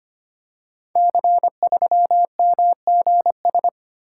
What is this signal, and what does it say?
Morse code, heard as a keyed tone, about 25 wpm, 700 Hz. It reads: C3MGS
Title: Code Groups - 5 Letter:Numbers - 25wpm